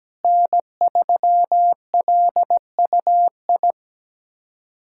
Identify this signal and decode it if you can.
Morse code, heard as a keyed tone, about 17 wpm, 700 Hz. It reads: N3LUI
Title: USA Callsign DE Callsign - 5 Characters - 17wpm